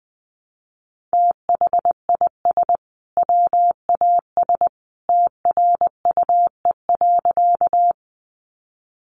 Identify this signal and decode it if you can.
Morse code, heard as a keyed tone, about 20 wpm, 700 Hz. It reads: THIS WAS TRUE.
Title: Sentences from Top 400 Words 20wpm